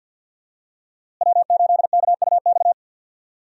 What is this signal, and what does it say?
Morse code, heard as a keyed tone, about 46 wpm, 700 Hz. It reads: W8KUX
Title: USA Call Signs - 5 Characters - Repeated 2x - 50wpm